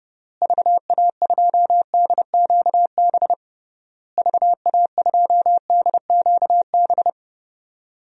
Morse code, heard as a keyed tone, about 30 wpm, 700 Hz. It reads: VA2DQB VA2DQB
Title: DXCC-100 Call Signs - 6 Characters - Repeated 3x - 30wpm